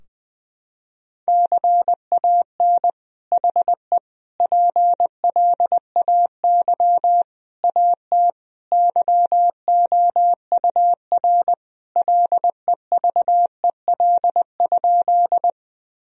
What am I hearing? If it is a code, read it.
Morse code, heard as a keyed tone, about 20 wpm, 700 Hz. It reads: CAN HE PLAY AT YOUR LEVEL?
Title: Sentences from Top 1000 Words - Review Part 4 - 20wpm